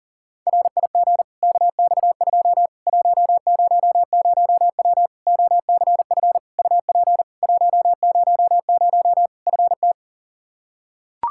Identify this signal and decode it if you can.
Morse code, heard as a keyed tone, about 40 wpm, 700 Hz. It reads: RIG KX2 100W OCF UP 100 FT
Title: Practice QSO - Exchange 2 - Info - 40wpm